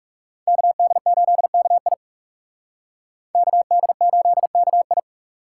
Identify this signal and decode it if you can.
Morse code, heard as a keyed tone, about 39 wpm, 700 Hz. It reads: KD8KI KD8KI